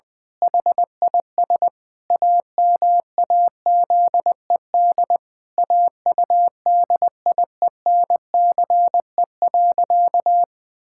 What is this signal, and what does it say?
Morse code, heard as a keyed tone, about 21 wpm, 700 Hz. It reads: HIS AMAZED AUDIENCE.
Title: Pangram Challenge v2 - 20wpm